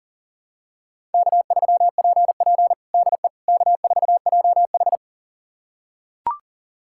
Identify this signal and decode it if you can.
Morse code, heard as a keyed tone, about 40 wpm, 690 Hz. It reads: K3PP DE K4JH
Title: USA Callsign DE Callsign - 4 Characters - 40wpm